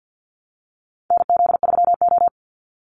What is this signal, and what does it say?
Morse code, heard as a keyed tone, about 50 wpm, 700 Hz. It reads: NZ3O
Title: Call Signs Repeated 2x - 50wpm